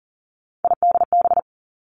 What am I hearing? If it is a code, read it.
Morse code, heard as a keyed tone, about 40 wpm, 700 Hz. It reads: IDB
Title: Three Challenging Letter-Number Combinations 40wpm